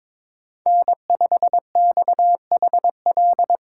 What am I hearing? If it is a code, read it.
Morse code, heard as a keyed tone, about 22 wpm, 700 Hz. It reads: N5XHL
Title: USA Call Signs - 2x Character Spacing - 22wpm